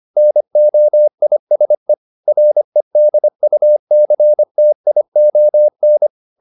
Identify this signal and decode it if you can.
Morse code, heard as a keyed tone, about 25 wpm, 600 Hz. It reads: NOISE REDUCTION